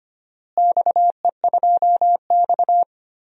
Morse code, heard as a keyed tone, about 25 wpm, 700 Hz. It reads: XE2X
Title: DXCC-100 Call Signs - 4 Characters - Repeated 2x - 25wpm